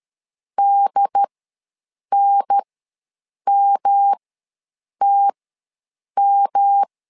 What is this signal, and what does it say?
Morse code, heard as a keyed tone, about 13 wpm, spaced as for 7 wpm, 800 Hz. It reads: DNMTM
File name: LEZIONI / 3 MWRO / 3 M ws 12wpm 1745